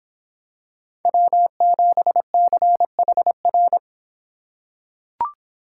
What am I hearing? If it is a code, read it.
Morse code, heard as a keyed tone, about 26 wpm, 700 Hz. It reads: W7CHR